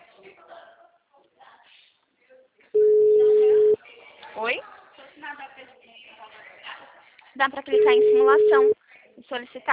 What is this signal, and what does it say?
Ringback tone (Europe, 425 Hz), with speech.